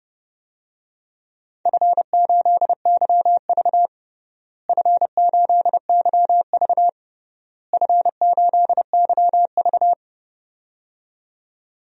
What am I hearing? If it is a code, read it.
Morse code, heard as a keyed tone, about 31 wpm, 700 Hz. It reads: F8YV F8YV F8YV
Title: DXCC-100 Call Signs - 4 Characters - Repeated 3x - 30wpm